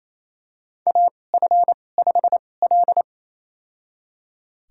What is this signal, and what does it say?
Morse code, heard as a keyed tone, about 28 wpm, spaced as for 20 wpm, 700 Hz. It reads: AF5L
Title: USA Call Signs - 4 Characters - 2x Character Spacing - 28wpm